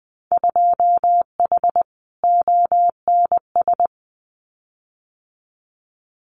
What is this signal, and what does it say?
Morse code, heard as a keyed tone, about 20 wpm, 700 Hz. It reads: 2H ONS
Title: Winter Field Day Exchanges Encore 20wpm